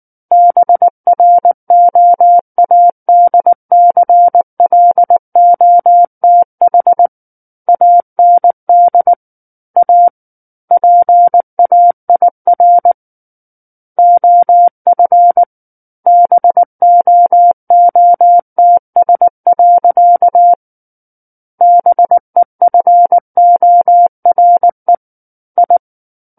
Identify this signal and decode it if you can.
Morse code, heard as a keyed tone, about 19 wpm, 700 Hz. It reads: BROADCLOTH AND A PAIR OF BOOTS. BEFORE I
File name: Book One / 10-Chapter 10 - 19 WPM